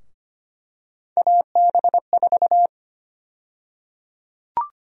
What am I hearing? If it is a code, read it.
Morse code, heard as a keyed tone, about 25 wpm, 700 Hz. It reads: AB4
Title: USA Call Sign Prefixes - Repeated 2x - 25wpm